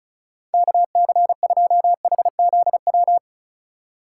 Morse code, heard as a keyed tone, about 36 wpm, 700 Hz. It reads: KC2HZW
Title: USA Call Signs - 2x Character Spacing - 35wpm